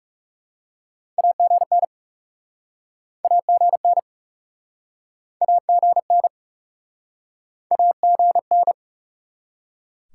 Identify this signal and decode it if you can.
Morse code, heard as a keyed tone, about 43 wpm, 700 Hz. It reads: AGN AGN AGN AGN